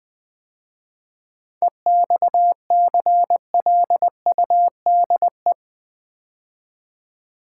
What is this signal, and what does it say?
Morse code, heard as a keyed tone, about 20 wpm, 700 Hz. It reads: EXCLUDE
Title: 7 Letter Words - First 3 Letters Given - 20wpm